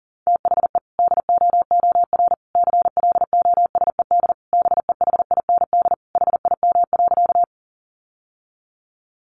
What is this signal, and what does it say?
Morse code, heard as a keyed tone, about 40 wpm, 700 Hz. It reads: THE DOOR CLOSED BEHIND HIM.
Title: Sentences from Top 1400 Words 40wpm